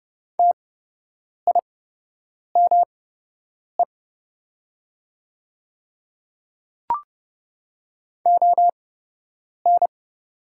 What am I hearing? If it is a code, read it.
Morse code, heard as a keyed tone, about 30 wpm, spaced as for 8 wpm, 700 Hz. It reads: TIME ON
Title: 8x Character Spacing - Top 100 Words - 30wpm